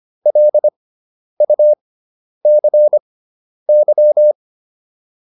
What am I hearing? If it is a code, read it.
Morse code, heard as a keyed tone, about 25 wpm, spaced as for 10 wpm, 600 Hz. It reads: LUCY